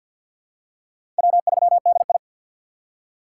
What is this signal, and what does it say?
Morse code, heard as a keyed tone, about 50 wpm, 700 Hz. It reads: W3DI
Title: USA Call Signs - 4 Characters - 6x Character Spacing - 50wpm